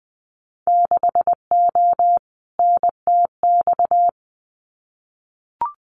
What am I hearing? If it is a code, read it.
Morse code, heard as a keyed tone, about 20 wpm, 700 Hz. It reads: 6O NTX
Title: Winter Field Day Exchanges Encore 20wpm